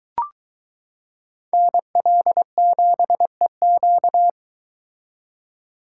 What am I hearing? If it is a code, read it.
Morse code, heard as a keyed tone, about 22 wpm, 700 Hz. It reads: NL7EQ